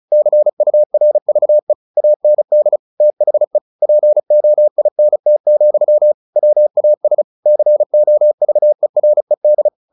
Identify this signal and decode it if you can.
Morse code, heard as a keyed tone, about 35 wpm, 600 Hz. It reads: CURVE AND THE POINT, WAS COVERED
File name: the old man and the sea / the old man and the sea 1 35wpm